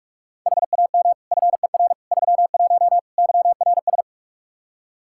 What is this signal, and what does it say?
Morse code, heard as a keyed tone, about 46 wpm, 700 Hz. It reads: HAM FER 31 YRS
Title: Practice QSO - Exchange 3 - More Info - 45wpm